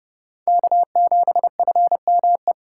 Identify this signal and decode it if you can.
Morse code, heard as a keyed tone, about 31 wpm, 700 Hz. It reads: K7FME